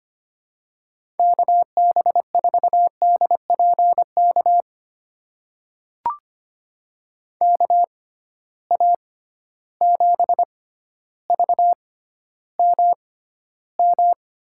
Morse code, heard as a keyed tone, about 25 wpm, spaced as for 13 wpm, 700 Hz. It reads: KB4DPK KA7VMM
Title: USA Call Signs - 6x Character Spacing - 25wpm